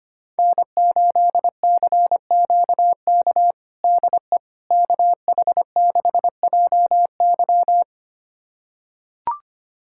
Morse code, heard as a keyed tone, about 25 wpm, 700 Hz. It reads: N8CQK DE KH6JY
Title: USA Callsign DE Callsign - 5 Characters - 25wpm